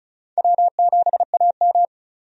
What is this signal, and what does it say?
Morse code, heard as a keyed tone, about 34 wpm, 700 Hz. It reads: W7AM
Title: USA Call Signs - 4 Characters - 35wpm